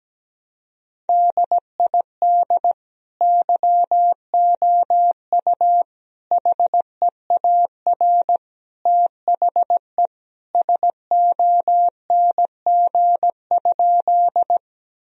Morse code, heard as a keyed tone, about 17 wpm, 700 Hz. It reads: DID YOU HEAR THE SONG?
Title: Sentences from Top 1000 Words - Review Part 2 - 17wpm